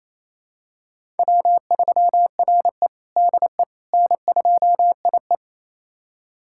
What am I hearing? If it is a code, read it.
Morse code, heard as a keyed tone, about 27 wpm, 700 Hz. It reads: W3RE DE N2IE
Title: USA Callsign DE Callsign - 4 Characters - 28wpm